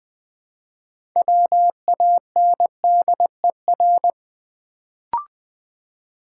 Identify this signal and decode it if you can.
Morse code, heard as a keyed tone, about 20 wpm, 700 Hz. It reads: WANDER
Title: Build A Word - 3 Letters At A Time - 6-Letter Words - 20wpm